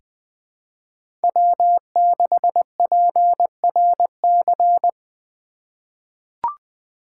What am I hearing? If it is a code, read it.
Morse code, heard as a keyed tone, about 20 wpm, 700 Hz. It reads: W6PRC
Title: USA Call Signs - 5 Characters - 2x Character Spacing - 20wpm